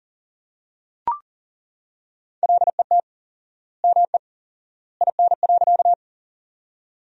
Morse code, heard as a keyed tone, about 40 wpm, 700 Hz. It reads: LET ME IN.
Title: 4x Word-Spacing - Sentences from Top 300 Words - 40wpm